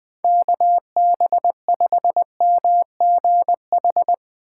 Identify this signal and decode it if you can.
Morse code, heard as a keyed tone, about 20 wpm, 700 Hz. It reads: KB5MGH
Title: USA Call Signs - 3x Character Spacing - 20wpm